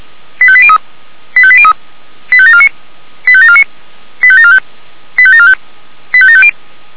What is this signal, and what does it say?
Five-tone selective call (ZVEI), heard as quick runs of five tones, tones of about 70 ms. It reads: 85803 85803 85749 85749 85745 85745 85750